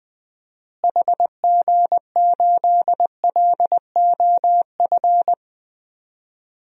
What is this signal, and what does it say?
Morse code, heard as a keyed tone, about 20 wpm, 700 Hz. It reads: HG8LOF
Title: DXCC-100 Call Signs - 6 Characters - Repeated 2x - 20wpm